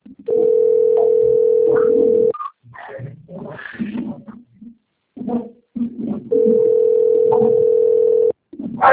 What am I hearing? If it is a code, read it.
Ringback tone (North America), 440 + 480 Hz.